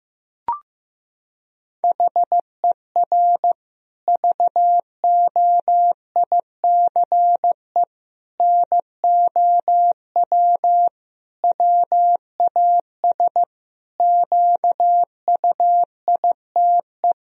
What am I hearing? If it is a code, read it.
Morse code, heard as a keyed tone, about 15 wpm, 700 Hz. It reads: HER VOICE NOW WAS QUITE
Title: Sentences from Top 1000 Words - Review Part 3 - 15wpm